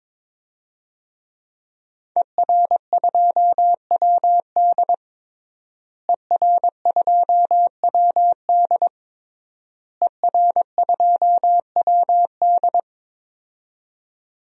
Morse code, heard as a keyed tone, about 22 wpm, 700 Hz. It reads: ER2WD ER2WD ER2WD
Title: DXCC-100 Call Signs - 5 Characters - Repeated 3x - 22wpm